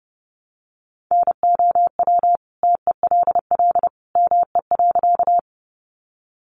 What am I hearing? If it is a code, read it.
Morse code, heard as a keyed tone, about 30 wpm, 700 Hz. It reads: NOW TELL ME.
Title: Sentences from Top 200 Words 30wpm